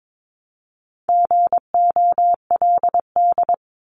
Morse code, heard as a keyed tone, about 22 wpm, 700 Hz. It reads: GOLD
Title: Top 500 Words 22wpm